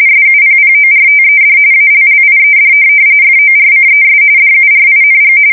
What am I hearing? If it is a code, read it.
RTTY, about 45.45 baud. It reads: SUPPORTED ALLIED FORCES DURING